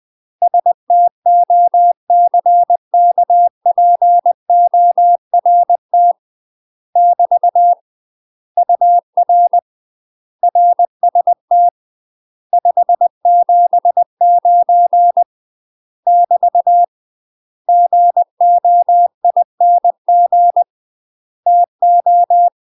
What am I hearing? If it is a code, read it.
Morse code, heard as a keyed tone, about 20 wpm, 700 Hz. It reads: STOCKPORT <BT> UR RST 579 <BT> GOING TO
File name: UK MASTER QSOs TRAINING / 20WPM / UK MASTER QSOs 0074